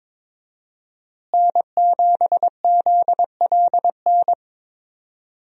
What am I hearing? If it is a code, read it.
Morse code, heard as a keyed tone, about 22 wpm, 700 Hz. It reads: N7ZLN
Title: USA Call Signs - 5 Characters - Repeated 3x - 22wpm